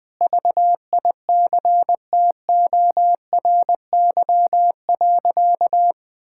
Morse code, heard as a keyed tone, about 20 wpm, 700 Hz. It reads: VICTORY.
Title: Sentences with 9-Word Phrases - 20wpm